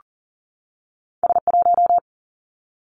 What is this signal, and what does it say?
Morse code, heard as a keyed tone, about 38 wpm, 700 Hz. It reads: S 1
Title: Two Challenging Letter-Number Combinations 40wpm